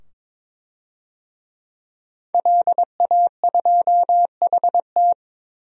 Morse code, heard as a keyed tone, about 22 wpm, 700 Hz. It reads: LA2HT
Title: DXCC-100 Call Signs - 5 Characters - 22wpm